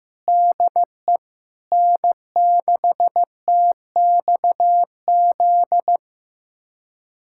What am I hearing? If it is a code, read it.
Morse code, heard as a keyed tone, about 15 wpm, 700 Hz. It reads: DE N6TXZ
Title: USA Callsign DE Callsign - 5 Characters - 15wpm